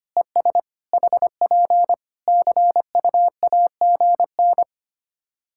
Morse code, heard as a keyed tone, about 25 wpm, 700 Hz. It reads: ES HP CUAGN
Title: Practice QSO - Exchange 4 - Ending - 25wpm